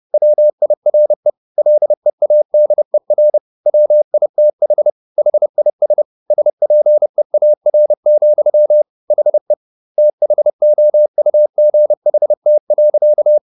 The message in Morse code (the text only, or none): WIRE LEADER WITH HIS SPEAR, HE THOUGHT.